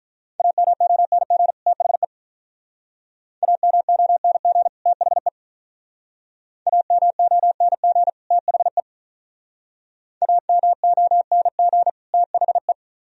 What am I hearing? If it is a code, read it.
Morse code, heard as a keyed tone, about 45 wpm, 700 Hz. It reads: AMONG THE AMONG THE AMONG THE AMONG THE